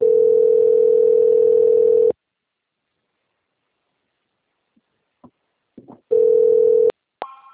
Ringback tone (North America), 440 + 480 Hz.